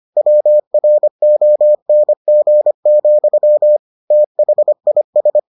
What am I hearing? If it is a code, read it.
Morse code, heard as a keyed tone, about 25 wpm, 600 Hz. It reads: WRONG, THIS